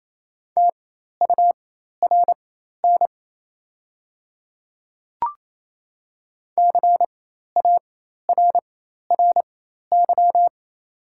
Morse code, heard as a keyed tone, about 27 wpm, spaced as for 13 wpm, 700 Hz. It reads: TURN CARRY
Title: 4x Character Spacing - Top 300 Words - 28wpm